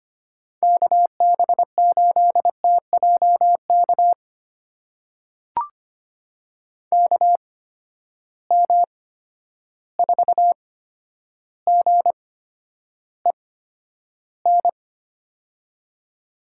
Morse code, heard as a keyed tone, about 25 wpm, spaced as for 11 wpm, 700 Hz. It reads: KB8TJK KM4GEN
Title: USA Call Signs - 6 Characters - 8x Character Spacing - 25wpm